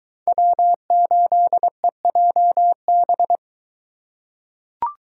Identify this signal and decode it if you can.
Morse code, heard as a keyed tone, about 22 wpm, 700 Hz. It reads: W8EJB